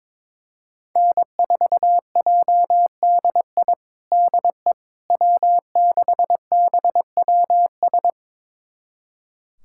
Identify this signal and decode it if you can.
Morse code, heard as a keyed tone, about 22 wpm, 700 Hz. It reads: N4JDI DE W6BWS
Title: USA Callsign DE Callsign - 5 Characters - 22wpm